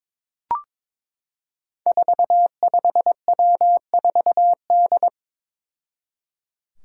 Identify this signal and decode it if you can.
Morse code, heard as a keyed tone, about 22 wpm, 700 Hz. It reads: 45W4D
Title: Five Challenging Letter-Number Combinations 22wpm